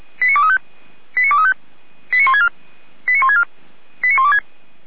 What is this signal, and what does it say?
Five-tone selective call (ZVEI), heard as quick runs of five tones, tones of about 70 ms. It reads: 89235 89235 89164 89164 89126
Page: ZVEI